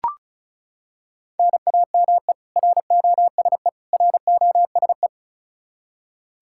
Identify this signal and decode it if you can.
Morse code, heard as a keyed tone, about 36 wpm, 700 Hz. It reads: NAME ROSE ROSE
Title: Practice QSO - Exchange 1 - Beginning - 35wpm